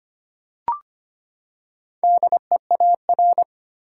Morse code, heard as a keyed tone, about 25 wpm, 700 Hz. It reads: DEAR
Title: Four Letter Words - 25wpm